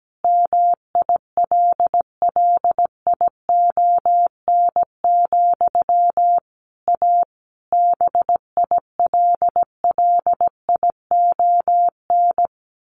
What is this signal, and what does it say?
Morse code, heard as a keyed tone, about 17 wpm, 700 Hz. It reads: MILLION, A BILLION
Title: A Sound Of Thunder 17wpm